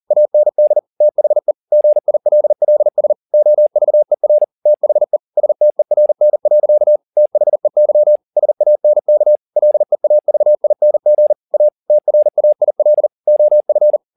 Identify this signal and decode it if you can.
Morse code, heard as a keyed tone, about 40 wpm, 600 Hz. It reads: AND THE GILLS OVER THE STERN. THEY SANK LEAVING A TRAIL OF
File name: the old man and the sea 2 40wpm